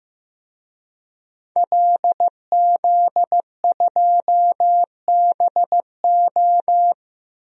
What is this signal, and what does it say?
Morse code, heard as a keyed tone, about 15 wpm, 700 Hz. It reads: LZ2BO
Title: DXCC-100 Call Signs - 5 Characters - 15wpm